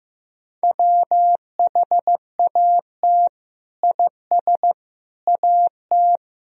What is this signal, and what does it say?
Morse code, heard as a keyed tone, about 15 wpm, 700 Hz. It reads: WHAT IS AT
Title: Sentences from Top 1000 Words - Review Part 2 - 15wpm